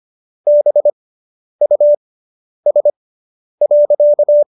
Morse code, heard as a keyed tone, about 25 wpm, spaced as for 10 wpm, 590 Hz. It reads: BUS.